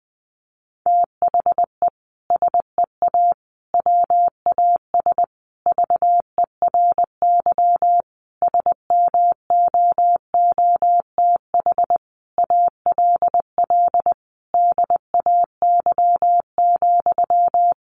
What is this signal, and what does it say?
Morse code, heard as a keyed tone, about 20 wpm, 700 Hz. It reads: THE SEA WAS VERY SMOOTH ALL DAY,
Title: Sentences from Top 1400 Words 20wpm